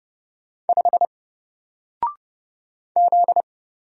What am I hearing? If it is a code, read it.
Morse code, heard as a keyed tone, about 30 wpm, 700 Hz. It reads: 5 Z
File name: Single-Letter-Number-30wpm